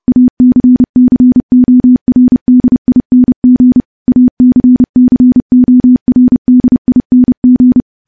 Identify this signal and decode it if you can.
Morse code, heard as a keyed tone, about 28 wpm, 250 Hz. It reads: ACCORDING ACCORDING
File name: morse-training-30WPM